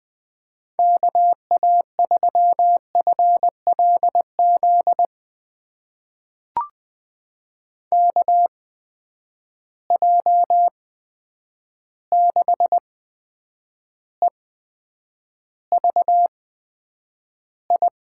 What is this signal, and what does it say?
Morse code, heard as a keyed tone, about 20 wpm, spaced as for 9 wpm, 700 Hz. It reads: KA3FLZ KJ6EVI
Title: USA Call Signs - 6 Characters - 8x Character Spacing - 20wpm